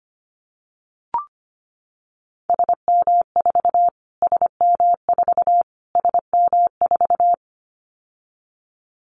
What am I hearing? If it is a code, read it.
Morse code, heard as a keyed tone, about 25 wpm, 700 Hz. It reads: SM4 SM4 SM4
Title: Callsign Prefixes for Top 25 DXCC - Repeated 3x - 25wpm